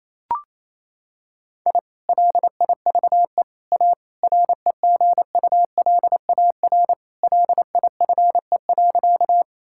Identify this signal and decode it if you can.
Morse code, heard as a keyed tone, about 28 wpm, 700 Hz. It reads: I LIVE A REGULAR LIFE.
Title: Sentences from Top 1000 Words - Review Part 6 - 28wpm